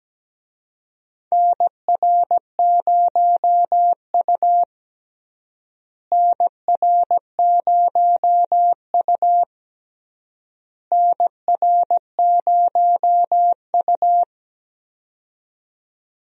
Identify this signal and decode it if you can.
Morse code, heard as a keyed tone, about 17 wpm, 700 Hz. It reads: NR0U NR0U NR0U
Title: USA Call Signs - 4 Characters - Repeated 3x - 17wpm